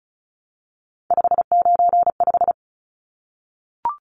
Morse code, heard as a keyed tone, about 34 wpm, 700 Hz. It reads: 595
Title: Numbers - 3-Digit 35wpm